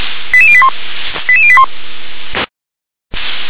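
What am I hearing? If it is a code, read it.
Five-tone selective call (ZVEI), heard as quick runs of five tones, tones of about 70 ms. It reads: 88081 88081
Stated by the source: ZVEI five-tone selective call 88148 received with an SDR.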